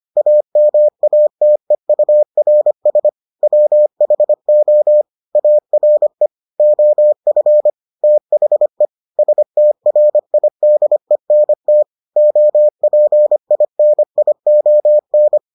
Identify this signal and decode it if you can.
Morse code, heard as a keyed tone, about 25 wpm, 600 Hz. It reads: AMATEURS WHO ARE OF THE STRIDENT OPINION